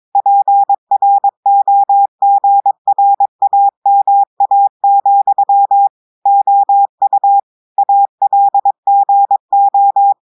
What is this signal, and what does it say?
Morse code, heard as a keyed tone, about 22 wpm, 800 Hz. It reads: PROGRAMA, OU ALGO
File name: audios / portugais / portugais texte en clair 2 vitesse 22 mots minutes